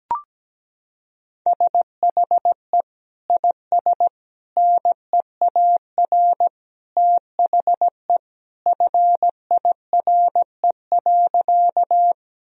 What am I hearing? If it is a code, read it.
Morse code, heard as a keyed tone, about 17 wpm, 700 Hz. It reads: SHE IS NEAR THE FIRE.
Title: Sentences from Top 1000 Words - Review Part 1 - 17wpm